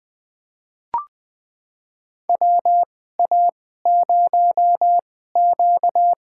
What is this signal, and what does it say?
Morse code, heard as a keyed tone, about 21 wpm, spaced as for 15 wpm, 700 Hz. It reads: WA0Q
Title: USA Call Signs - 4 Characters - 2x Character Spacing - 20wpm